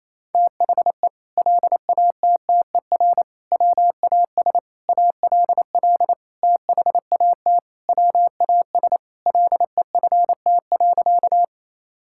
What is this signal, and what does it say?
Morse code, heard as a keyed tone, about 28 wpm, 700 Hz. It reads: THE LATTER WAS ALL THAT WAS LEFT.
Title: Sentences from Top 1600 Words 28wpm